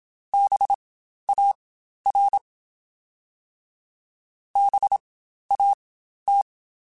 Morse code, heard as a keyed tone, about 27 wpm, spaced as for 12 wpm, 800 Hz. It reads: BAR BAT
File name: Spaced Alphas / Spaced Alphas mp3s / Bs sp @25